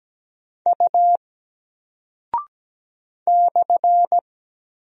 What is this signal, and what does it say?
Morse code, heard as a keyed tone, about 17 wpm, 700 Hz. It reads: U /
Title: Single Character (Letter:Number:Punctuation:Prosign) Alt 17wpm